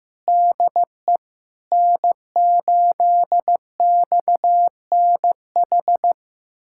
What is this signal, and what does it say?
Morse code, heard as a keyed tone, about 15 wpm, 700 Hz. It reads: DE N8XNH
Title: USA Callsign DE Callsign - 5 Characters - 15wpm